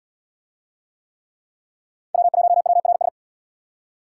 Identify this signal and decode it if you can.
Morse code, heard as a keyed tone, about 69 wpm, 700 Hz. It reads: W1WRS